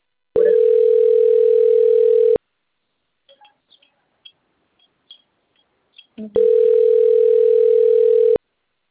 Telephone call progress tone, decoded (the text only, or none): ringback tone